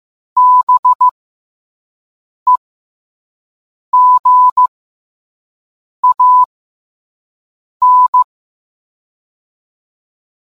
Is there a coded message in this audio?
Yes: Morse code from 0.4 to 8.2 s, heard as a keyed tone, about 15 wpm, spaced as for 5 wpm, 1000 Hz. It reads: BEGAN